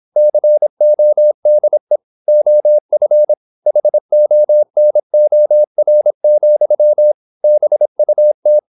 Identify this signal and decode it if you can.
Morse code, heard as a keyed tone, about 26 wpm, 600 Hz. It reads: CODE OF HONOR, BUT
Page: Never 26wpm | CW med Gnister